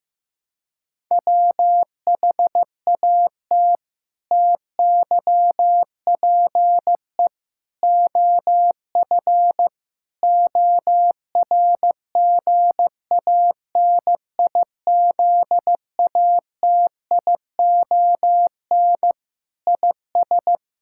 Morse code, heard as a keyed tone, about 15 wpm, 700 Hz. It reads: WHAT TYPE OF ORGANIZATION IS
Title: Sentences from Top 1000 Words - Review Part 5 - 15wpm